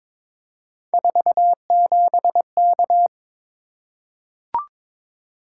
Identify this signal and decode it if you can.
Morse code, heard as a keyed tone, about 23 wpm, 700 Hz. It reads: 47K
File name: practice / 22wpm / Three Letter-Number Combinations - 22wpm